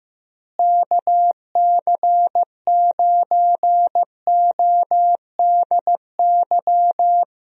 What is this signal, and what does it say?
Morse code, heard as a keyed tone, about 15 wpm, 700 Hz. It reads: KC9ODY
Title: USA Call Signs - 6 Characters - Repeated 3x - 15wpm